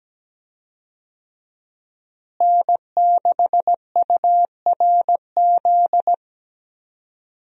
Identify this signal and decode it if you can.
Morse code, heard as a keyed tone, about 17 wpm, 700 Hz. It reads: N6URZ
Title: USA Call Signs - 5 Characters - Repeated 2x - 17wpm